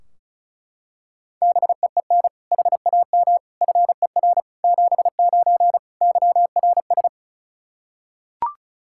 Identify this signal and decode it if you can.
Morse code, heard as a keyed tone, about 34 wpm, 700 Hz. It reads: BEEN HAM FER 79 YRS
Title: Practice QSO - Exchange 3 - More Info - 35wpm